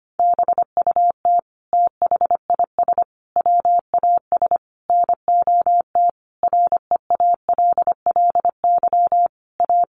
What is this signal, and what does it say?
Morse code, heard as a keyed tone, about 25 wpm, 700 Hz. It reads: BUT THIS WAS NOT REALLY A
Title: Sentences from Top 1500 Words 25wpm